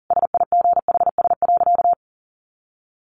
Morse code, heard as a keyed tone, about 40 wpm, 700 Hz. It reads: SIGHS.
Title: A Sound Of Thunder 40wpm